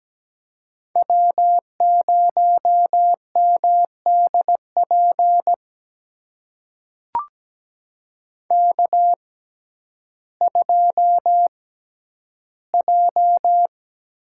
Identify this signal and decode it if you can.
Morse code, heard as a keyed tone, about 17 wpm, spaced as for 10 wpm, 700 Hz. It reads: W0MDP K2J
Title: USA Call Signs - 5 Characters - 6x Character Spacing - 17wpm